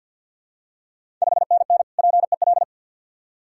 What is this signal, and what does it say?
Morse code, heard as a keyed tone, about 50 wpm, 700 Hz. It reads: 5NN PEL
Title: Ontario QSO Party Exchange 50wpm